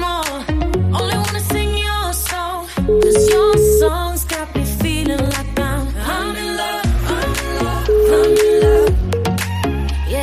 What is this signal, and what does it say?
Ringback tone (Europe, 425 Hz), with noise.